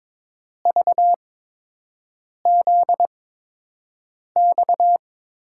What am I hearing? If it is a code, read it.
Morse code, heard as a keyed tone, about 22 wpm, spaced as for 6 wpm, 700 Hz. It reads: VZX
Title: USA Call Signs - 5 Characters - 8x Character Spacing - 22wpm